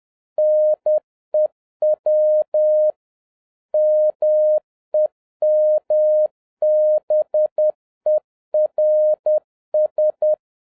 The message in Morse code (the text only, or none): NEW MEMBERS